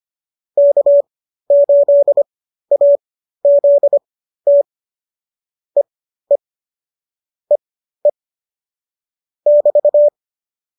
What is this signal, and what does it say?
Morse code, heard as a keyed tone, about 25 wpm, spaced as for 13 wpm, 570 Hz. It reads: K8AZT EE EE <BT>